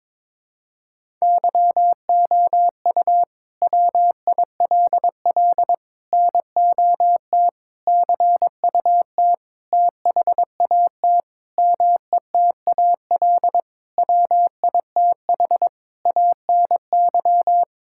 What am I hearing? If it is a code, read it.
Morse code, heard as a keyed tone, about 22 wpm, 700 Hz. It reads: YOU WILL NOT CUT THAT METAL WITH ANY
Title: Sentences from Top 1000 Words - Review Part 6 - 22wpm